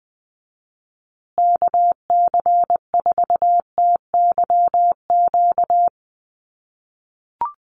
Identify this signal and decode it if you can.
Morse code, heard as a keyed tone, about 20 wpm, 700 Hz. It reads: KC4TYQ
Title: USA Call Signs Repeated 3x 20wpm